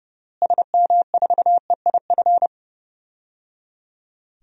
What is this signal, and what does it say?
Morse code, heard as a keyed tone, about 30 wpm, 700 Hz. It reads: SM4EIF
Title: DXCC-100 Call Signs - 6 Characters - Repeated 3x - 30wpm